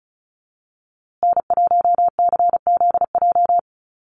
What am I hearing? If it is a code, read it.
Morse code, heard as a keyed tone, about 34 wpm, 700 Hz. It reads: N1CZJ
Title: USA Call Signs Repeated 3x 35wpm